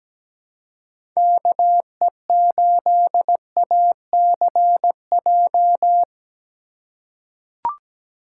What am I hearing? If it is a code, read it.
Morse code, heard as a keyed tone, about 17 wpm, 700 Hz. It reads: KE8ACJ
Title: USA Call Signs - 3x Character Spacing - 17wpm